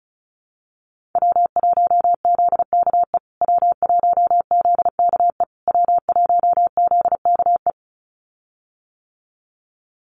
Morse code, heard as a keyed tone, about 34 wpm, 700 Hz. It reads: W1ZKE W1ZKE W1ZKE
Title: USA Call Signs Repeated 3x 35wpm